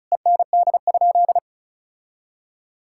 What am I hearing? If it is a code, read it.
Morse code, heard as a keyed tone, about 36 wpm, 700 Hz. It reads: END?
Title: Sentences from Top 1000 Words - Review Part 5 - 35wpm